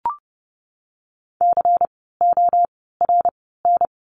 Morse code, heard as a keyed tone, about 30 wpm, 700 Hz. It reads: C O R N
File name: Lesson 16f - Letter c - Words 30wpm